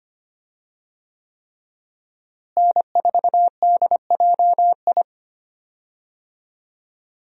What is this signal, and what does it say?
Morse code, heard as a keyed tone, about 25 wpm, 700 Hz. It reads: N4DJI
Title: USA Call Signs - 5 Characters - Repeated 3x - 25wpm